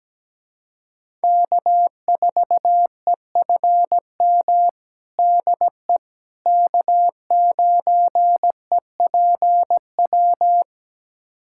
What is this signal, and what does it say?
Morse code, heard as a keyed tone, about 17 wpm, 700 Hz. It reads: K4EFM DE K9EPW
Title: USA Callsign DE Callsign - 5 Characters - 17wpm